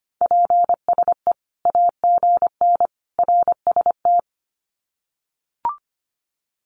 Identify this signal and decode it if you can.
Morse code, heard as a keyed tone, about 25 wpm, 700 Hz. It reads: PSE AGN RST
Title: Common QSO Phrases 25wpm